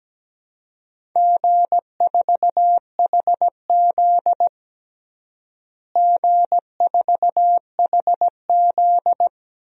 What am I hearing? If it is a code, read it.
Morse code, heard as a keyed tone, about 17 wpm, 700 Hz. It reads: G4HZ G4HZ
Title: DXCC-100 Call Signs - 4 Characters - Repeated 3x - 17wpm